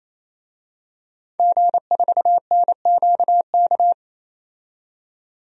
Morse code, heard as a keyed tone, about 28 wpm, 700 Hz. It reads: G4NQK
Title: DXCC-100 Call Signs - 5 Characters - 28wpm